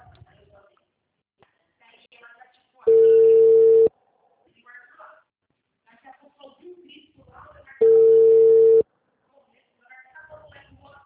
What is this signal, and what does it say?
Ringback tone (China, 450 Hz), with speech.
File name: out- -2020- - - .90880